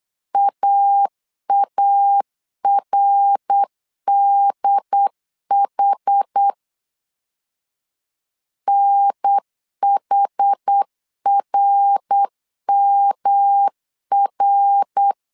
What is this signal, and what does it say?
Morse code, heard as a keyed tone, about 8 wpm, 800 Hz. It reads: AARDH NHRMR